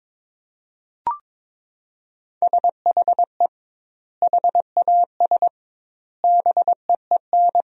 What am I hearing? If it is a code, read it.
Morse code, heard as a keyed tone, about 22 wpm, 700 Hz. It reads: SHE HAS BEEN
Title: 2x Word-Spacing - Sentences from Top 200 Words - 22wpm